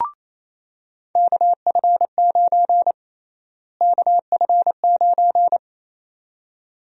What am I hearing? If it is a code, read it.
Morse code, heard as a keyed tone, about 28 wpm, 700 Hz. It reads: KF9 KF9
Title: USA Call Sign Prefixes - Repeated 2x - 28wpm